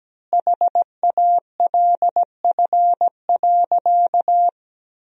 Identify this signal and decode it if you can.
Morse code, heard as a keyed tone, about 17 wpm, 700 Hz. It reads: HALF.
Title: 3x Word-Spacing - Sentences from Top 400 Words - 17wpm